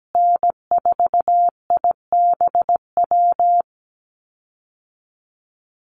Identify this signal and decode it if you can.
Morse code, heard as a keyed tone, about 17 wpm, 700 Hz. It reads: N4IBW
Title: USA Call Signs Repeated 3x 17wpm